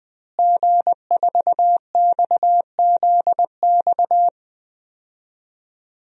Morse code, heard as a keyed tone, about 20 wpm, 700 Hz. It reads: G4XZX
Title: DXCC-100 Call Signs - 5 Characters - Repeated 3x - 20wpm